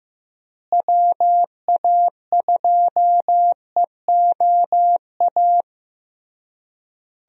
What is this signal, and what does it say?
Morse code, heard as a keyed tone, about 16 wpm, 700 Hz. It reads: WA2EOA
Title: USA Call Signs - 6 Characters - Repeated 3x - 15wpm